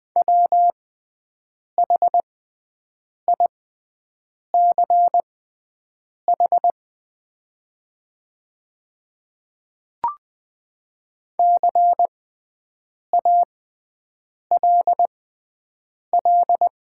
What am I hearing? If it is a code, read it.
Morse code, heard as a keyed tone, about 20 wpm, spaced as for 7 wpm, 700 Hz. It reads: WHICH CALL
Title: 6x Character Spacing - Top 100 Words - 20wpm